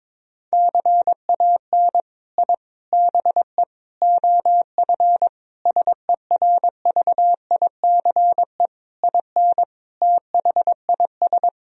Morse code, heard as a keyed tone, about 22 wpm, 700 Hz. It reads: CAN I BE OF SERVICE IN THIS
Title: Sentences from Top 1000 Words - Review Part 4 - 22wpm